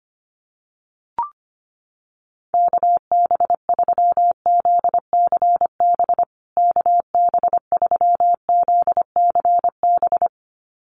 Morse code, heard as a keyed tone, about 25 wpm, 700 Hz. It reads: KB3ZCB KB3ZCB
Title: USA Call Signs Repeated 2x 25wpm